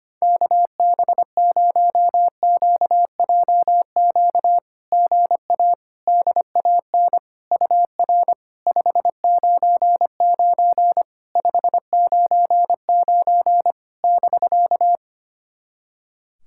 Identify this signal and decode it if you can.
Morse code, heard as a keyed tone, about 25 wpm, 700 Hz. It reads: KB0QJQ GA DAN UR 599 599 <BK>
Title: SOTA QSO Practice 25wpm